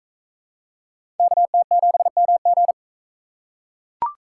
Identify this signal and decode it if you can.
Morse code, heard as a keyed tone, about 37 wpm, 700 Hz. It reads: KT7MG